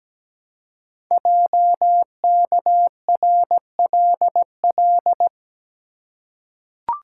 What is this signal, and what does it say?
Morse code, heard as a keyed tone, about 17 wpm, 700 Hz. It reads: JKRLL
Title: Code Groups - 5 Letters - 17wpm